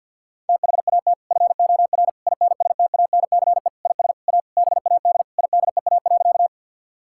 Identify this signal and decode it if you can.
Morse code, heard as a keyed tone, about 50 wpm, 700 Hz. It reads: THAT FOR INSTANCE IS A BAD IDEA.
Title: Sentences from Top 1000 Words - Review Part 5 - 50wpm